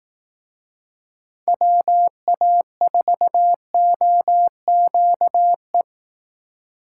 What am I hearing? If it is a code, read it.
Morse code, heard as a keyed tone, about 18 wpm, 700 Hz. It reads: WA4OQE
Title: Speed Racing - USA Call Signs 15wpm